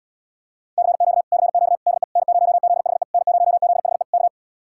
Morse code, heard as a keyed tone, about 79 wpm, 700 Hz. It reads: CQ CQ DE N0ZHE N0ZHE K